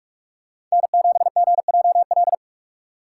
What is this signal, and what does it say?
Morse code, heard as a keyed tone, about 46 wpm, 700 Hz. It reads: N7GJL
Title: USA Call Signs - 5 Characters - 2x Character Spacing - 45wpm